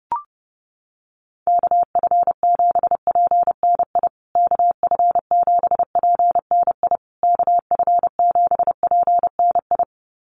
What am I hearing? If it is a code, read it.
Morse code, heard as a keyed tone, about 30 wpm, 700 Hz. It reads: KF7PNI KF7PNI KF7PNI
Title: USA Call Signs Repeated 3x 30wpm